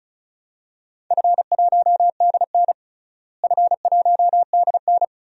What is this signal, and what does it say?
Morse code, heard as a keyed tone, about 36 wpm, 700 Hz. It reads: F1DN F1DN
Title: DXCC-100 Call Signs - 4 Characters - Repeated 2x - 35wpm